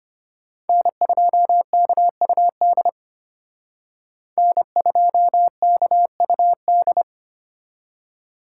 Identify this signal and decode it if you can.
Morse code, heard as a keyed tone, about 32 wpm, 700 Hz. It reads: N2KUD N2KUD